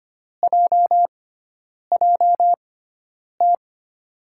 Morse code, heard as a keyed tone, about 25 wpm, spaced as for 9 wpm, 700 Hz. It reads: JJT
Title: USA Call Signs - 6 Characters - 6x Character Spacing - 25wpm